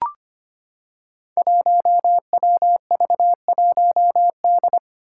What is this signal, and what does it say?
Morse code, heard as a keyed tone, about 25 wpm, 700 Hz. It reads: 1WV1D
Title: Five Challenging Letter-Number Combinations 25wpm